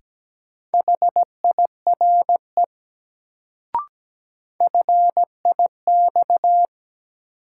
Morse code, heard as a keyed tone, about 17 wpm, 700 Hz. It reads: HIRE FIX
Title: Sentences from Top 1400 Words - New Word Drill - 17wpm